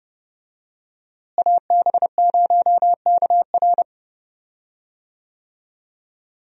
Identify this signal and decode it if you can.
Morse code, heard as a keyed tone, about 30 wpm, 700 Hz. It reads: AB0KR
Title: USA Call Signs - 5 Characters - Repeated 2x - 30wpm